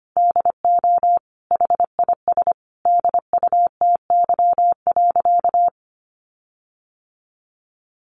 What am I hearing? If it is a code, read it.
Morse code, heard as a keyed tone, about 25 wpm, 700 Hz. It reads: DO HIS DUTY.
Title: Sentences from Top 1500 Words 25wpm